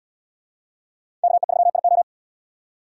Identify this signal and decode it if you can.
Morse code, heard as a keyed tone, about 69 wpm, 700 Hz. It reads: KE3IO